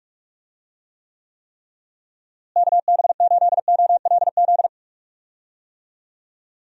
Morse code, heard as a keyed tone, about 46 wpm, 700 Hz. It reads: KB8OLZ
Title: USA Call Signs - 6 Characters - Repeated 3x - 45wpm